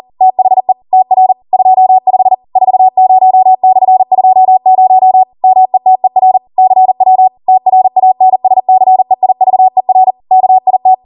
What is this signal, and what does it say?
Morse code, heard as a keyed tone, about 40 wpm, 750 Hz. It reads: THE TR 25 40/20 METER CW TRANSCEIVER KIT
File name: Morse / 40WPM